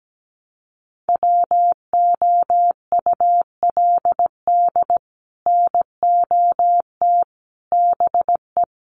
Morse code, heard as a keyed tone, about 17 wpm, 700 Hz. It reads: WOULD NOT BE
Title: Sets of 3 Words 17wpm